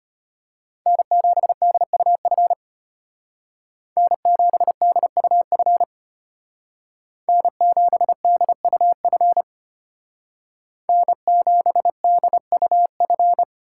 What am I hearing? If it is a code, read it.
Morse code, heard as a keyed tone, about 33 wpm, 700 Hz. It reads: N7DUF N7DUF N7DUF N7DUF